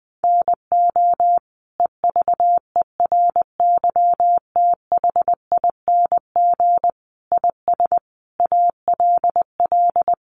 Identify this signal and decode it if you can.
Morse code, heard as a keyed tone, about 20 wpm, 700 Hz. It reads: NO EVERYTHING IS ALL
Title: Sentences from Top 700 Words 20wpm